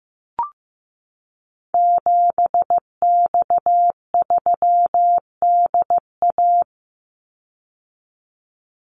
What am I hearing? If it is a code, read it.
Morse code, heard as a keyed tone, about 15 wpm, 700 Hz. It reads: 7X3DA
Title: Call Signs Repeated 3x - 15wpm